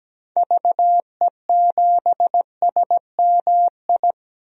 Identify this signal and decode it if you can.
Morse code, heard as a keyed tone, about 17 wpm, 700 Hz. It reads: VE7SMI
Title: DXCC-100 Call Signs - 6 Characters - Repeated 2x - 17wpm